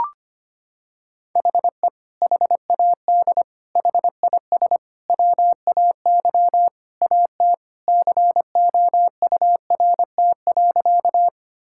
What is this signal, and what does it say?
Morse code, heard as a keyed tone, about 25 wpm, 700 Hz. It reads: HE HAD HIS WAY AT COURT.
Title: Sentences from Top 1600 Words 25wpm